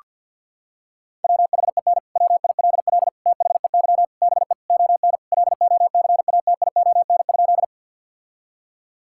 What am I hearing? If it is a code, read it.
Morse code, heard as a keyed tone, about 50 wpm, 700 Hz. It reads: WHEN WILL THEY BE ON LOCATION?
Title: Sentences from Top 1000 Words - Review Part 6 - 50wpm